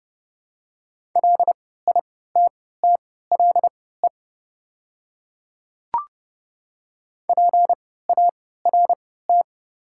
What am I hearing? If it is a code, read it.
Morse code, heard as a keyed tone, about 30 wpm, spaced as for 17 wpm, 700 Hz. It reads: LITTLE PART
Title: 3x Character Spacing - Top 200 Words - 30wpm